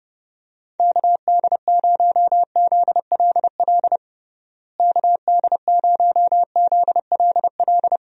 Morse code, heard as a keyed tone, about 30 wpm, 700 Hz. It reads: KD0ZLL KD0ZLL
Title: USA Call Signs - 6 Characters - Repeated 2x - 30wpm